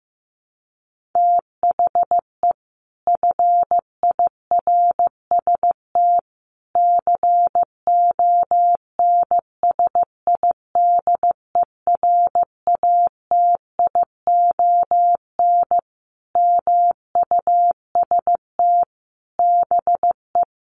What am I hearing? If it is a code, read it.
Morse code, heard as a keyed tone, about 14 wpm, 700 Hz. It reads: THE FIRST CONSIDERATION MUST BE
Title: Sentences from Top 1500 Words 15wpm